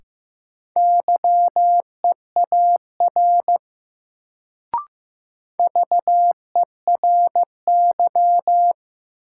Morse code, heard as a keyed tone, about 15 wpm, 700 Hz. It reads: YEAR VERY
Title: Sets of 2 Words - New Word Drill - 15wpm